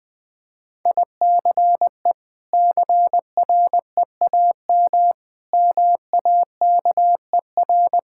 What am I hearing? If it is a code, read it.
Morse code, heard as a keyed tone, about 20 wpm, 700 Hz. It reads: ICE CREAM MAKER
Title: Difficult Words - Speed-Racing with Hints - 20wpm